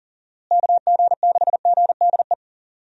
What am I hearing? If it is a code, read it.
Morse code, heard as a keyed tone, about 40 wpm, 700 Hz. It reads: KG6GDE
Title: USA Call Signs - 6 Characters - Repeated 2x - 40wpm